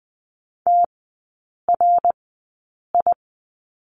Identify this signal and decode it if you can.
Morse code, heard as a keyed tone, about 20 wpm, spaced as for 8 wpm, 700 Hz. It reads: TRI
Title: 060 Lesson 11e - Letter h - Three Characters 20wpm